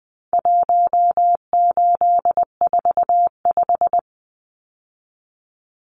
Morse code, heard as a keyed tone, about 20 wpm, 700 Hz. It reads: 1845
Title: Numbers - 4-Digit 20wpm